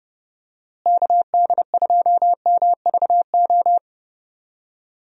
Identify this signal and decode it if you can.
Morse code, heard as a keyed tone, about 31 wpm, 700 Hz. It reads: KD2MVO
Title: USA Call Signs - 6 Characters - Repeated 2x - 30wpm